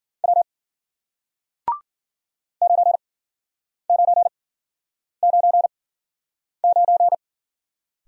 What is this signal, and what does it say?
Morse code, heard as a keyed tone, about 53 wpm, spaced as for 9 wpm, 700 Hz. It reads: W 9999